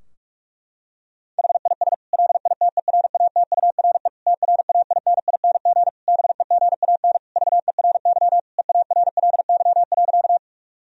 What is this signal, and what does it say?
Morse code, heard as a keyed tone, about 42 wpm, 700 Hz. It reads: HIS LITERATURE TRAINING BEGAN VERY EARLY.
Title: Sentences from Top 1000 Words - Review Part 4 - 45wpm